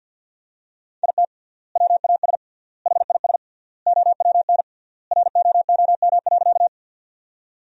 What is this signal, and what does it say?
Morse code, heard as a keyed tone, about 49 wpm, 700 Hz. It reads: IT WAS HIS OWN ROOM.
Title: 3x Word-Spacing - Sentences from Top 300 Words - 50wpm